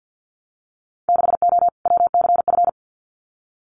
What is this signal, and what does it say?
Morse code, heard as a keyed tone, about 50 wpm, 700 Hz. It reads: 6O WCF
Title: Winter Field Day Exchanges Encore 50wpm